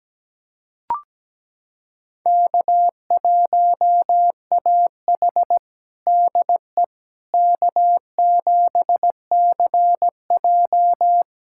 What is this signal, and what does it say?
Morse code, heard as a keyed tone, about 17 wpm, 700 Hz. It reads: K1AH DE K7CJ
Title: USA Callsign DE Callsign - 4 Characters - 17wpm